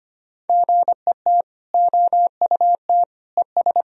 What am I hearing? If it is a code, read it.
Morse code, heard as a keyed tone, about 25 wpm, 700 Hz. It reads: GET OUT ES
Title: Sentences from Top 200 Words and Abbreviations 25wpm